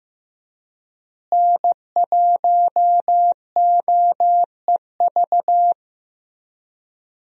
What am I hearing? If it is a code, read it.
Morse code, heard as a keyed tone, about 15 wpm, 700 Hz. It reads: N1OEV
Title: USA Call Signs - 5 Characters - Repeated 3x - 15wpm